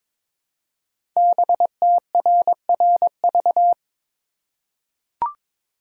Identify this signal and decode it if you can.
Morse code, heard as a keyed tone, about 22 wpm, 690 Hz. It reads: BTRRV
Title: Code Groups - 5 Letters - 22wpm